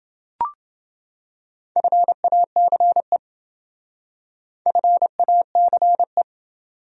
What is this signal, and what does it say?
Morse code, heard as a keyed tone, about 27 wpm, 700 Hz. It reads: FACE FACE